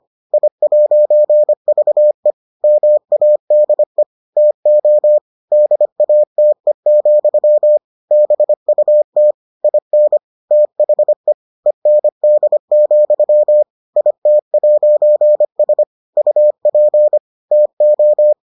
Morse code, heard as a keyed tone, about 25 wpm, 600 Hz. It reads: I'VE MADE TO DATE, BUT IN THE END, IT'S UP TO